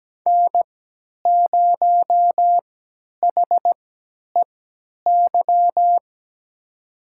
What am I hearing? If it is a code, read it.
Morse code, heard as a keyed tone, about 17 wpm, spaced as for 10 wpm, 700 Hz. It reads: N0HEY
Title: USA Call Signs - 5 Characters - 3x Character Spacing - 17wpm